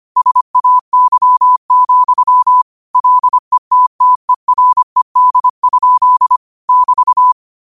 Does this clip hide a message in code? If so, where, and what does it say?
Yes: Morse code from 0.2 to 7.3 s, heard as a keyed tone, about 25 wpm, 1000 Hz. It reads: SAY, LETTERED? <BT>